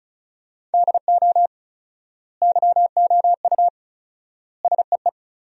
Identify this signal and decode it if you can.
Morse code, heard as a keyed tone, about 34 wpm, 700 Hz. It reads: DO YOU SEE
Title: 4x Word-Spacing - Sentences from Top 100 Words - 35wpm